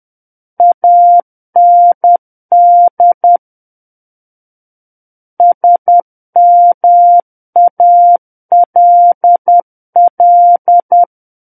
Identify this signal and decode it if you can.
Morse code, heard as a keyed tone, about 10 wpm, 700 Hz. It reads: AND SMALL
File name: Book One / 01-Chapter 1 - 10 WPM